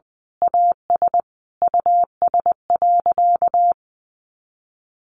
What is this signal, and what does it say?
Morse code, heard as a keyed tone, about 20 wpm, 700 Hz. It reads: AS US.
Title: Sentences from Top 500 Words 20wpm